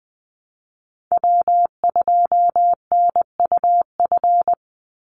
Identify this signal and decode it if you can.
Morse code, heard as a keyed tone, about 20 wpm, 700 Hz. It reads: W2NUF
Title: USA Call Signs 20wpm